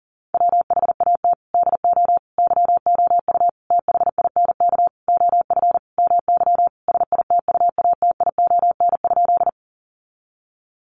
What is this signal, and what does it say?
Morse code, heard as a keyed tone, about 40 wpm, 700 Hz. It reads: WHAT DO YOU THINK OF MY SITUATION?
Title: Sentences from Top 1200 Words 40wpm